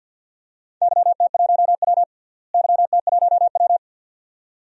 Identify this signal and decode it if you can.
Morse code, heard as a keyed tone, about 53 wpm, 700 Hz. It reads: YT1W YT1W
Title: DXCC-100 Call Signs - 4 Characters - Repeated 2x - 50wpm